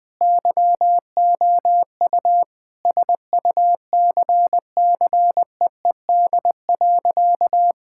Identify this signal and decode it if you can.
Morse code, heard as a keyed tone, about 20 wpm, 700 Hz. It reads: YOU SUCCEED.
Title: Sentences with 2-Word Phrases - 20wpm